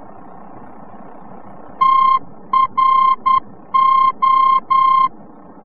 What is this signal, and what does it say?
Morse code, heard as a keyed tone, about 10 wpm, 1060 Hz. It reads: TRO
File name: france / mx / TRO-VOR-F